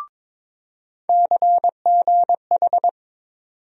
Morse code, heard as a keyed tone, about 22 wpm, 700 Hz. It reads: CGH
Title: USA Call Sign Postfixes - 3 Letters 22wpm